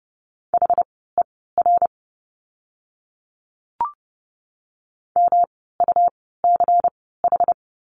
Morse code, heard as a keyed tone, about 30 wpm, spaced as for 17 wpm, 700 Hz. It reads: HER MUCH
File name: course / 30:10wpm / 115 Lesson 19f - Letter w - Words 30wpm